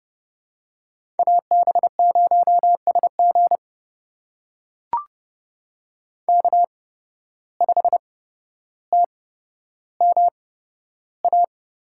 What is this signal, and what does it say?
Morse code, heard as a keyed tone, about 30 wpm, spaced as for 13 wpm, 700 Hz. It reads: AB0SG K5TMA
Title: USA Call Signs - 5 Characters - 8x Character Spacing - 30wpm